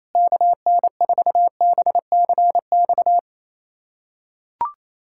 Morse code, heard as a keyed tone, about 28 wpm, 700 Hz. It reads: KN4BCX
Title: USA Callsign DE Callsign 28wpm